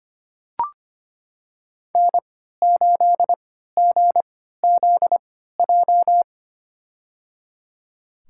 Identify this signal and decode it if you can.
Morse code, heard as a keyed tone, about 25 wpm, spaced as for 14 wpm, 700 Hz. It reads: N8GZJ
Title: USA Call Signs - 5 Characters - 3x Character Spacing - 25wpm